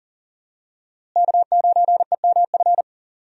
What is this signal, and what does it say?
Morse code, heard as a keyed tone, about 40 wpm, 700 Hz. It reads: K9EMF
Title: USA Call Signs - 5 Characters - Repeated 2x - 40wpm